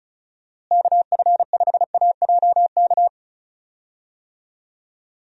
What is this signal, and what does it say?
Morse code, heard as a keyed tone, about 36 wpm, 700 Hz. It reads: KF5AJK
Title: USA Call Signs - 6 Characters - Repeated 2x - 35wpm